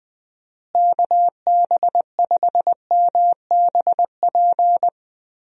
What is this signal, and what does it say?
Morse code, heard as a keyed tone, about 20 wpm, 700 Hz. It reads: KB5MBP
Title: USA Call Signs - 6 Characters - Repeated 2x - 20wpm